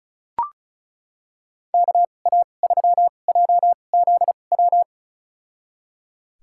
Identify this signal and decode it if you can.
Morse code, heard as a keyed tone, about 36 wpm, spaced as for 25 wpm, 700 Hz. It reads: KA3JZW
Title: USA Call Signs - 6 Characters - 2x Character Spacing - 35wpm